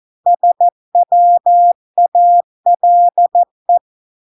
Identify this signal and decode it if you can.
Morse code, heard as a keyed tone, about 14 wpm, 700 Hz. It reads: SWALE